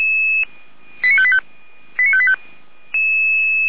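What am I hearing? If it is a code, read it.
Five-tone selective call (ZVEI), heard as quick runs of five tones, tones of about 70 ms. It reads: 89585 89585
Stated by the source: ZVEI five-tone call 89585 received on SDR.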